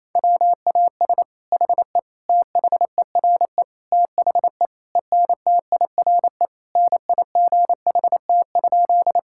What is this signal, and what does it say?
Morse code, heard as a keyed tone, about 28 wpm, 700 Hz. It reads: WAS HE THERE THE ENTIRE NIGHT?
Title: Sentences from Top 1000 Words - Review Part 5 - 28wpm